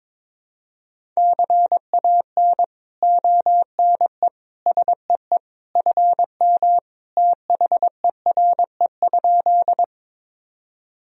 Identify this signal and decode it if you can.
Morse code, heard as a keyed tone, about 21 wpm, 700 Hz. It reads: CAN ONE SEE FM THERE?
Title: Sentences from Top 100 Words and Abbreviations 22wpm